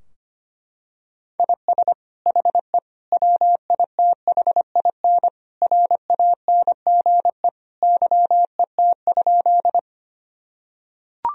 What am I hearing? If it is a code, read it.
Morse code, heard as a keyed tone, about 25 wpm, 700 Hz. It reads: IS HE WITHIN RANGE YET?
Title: Sentences from Top 1000 Words - Review Part 4 - 25wpm